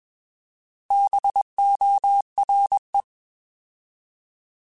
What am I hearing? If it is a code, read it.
Morse code, heard as a keyed tone, about 21 wpm, 800 Hz. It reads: BORE
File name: Bs @20